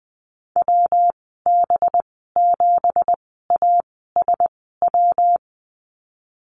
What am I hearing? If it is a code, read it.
Morse code, heard as a keyed tone, about 20 wpm, spaced as for 15 wpm, 700 Hz. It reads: WB7ASW
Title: 193 Lesson 31g - Letter v - Call Signs 20wpm